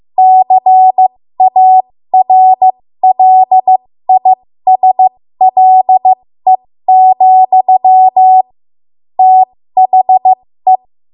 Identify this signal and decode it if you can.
Morse code, heard as a keyed tone, about 15 wpm, spaced as for 13 wpm, 750 Hz. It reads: CARLISLE, THE
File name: Morse / 13WPM